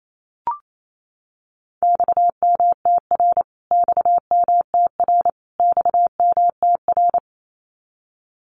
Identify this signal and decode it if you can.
Morse code, heard as a keyed tone, about 28 wpm, 700 Hz. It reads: XMTR XMTR XMTR
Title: Abbreviations Repeated 3x - 28wpm